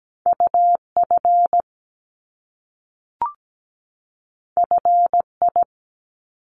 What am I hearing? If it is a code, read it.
Morse code, heard as a keyed tone, about 17 wpm, 700 Hz. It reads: UF FI
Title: Top Two Letter Combinations in Words 17wpm